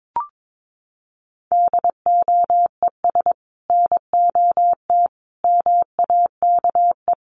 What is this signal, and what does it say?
Morse code, heard as a keyed tone, about 22 wpm, 700 Hz. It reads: DOES NOT MAKE
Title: Sets of 5 Words Encore2 22wpm